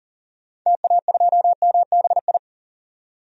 Morse code, heard as a keyed tone, about 40 wpm, 700 Hz. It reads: TA2MBI
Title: DXCC-100 Call Signs - 6 Characters - 40wpm